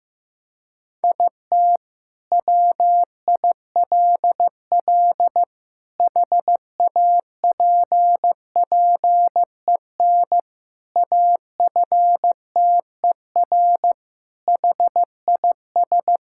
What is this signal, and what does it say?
Morse code, heard as a keyed tone, about 15 wpm, 700 Hz. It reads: IT WILL HAPPEN AFTER HIS
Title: Sentences from Top 1000 Words - Review Part 5 - 15wpm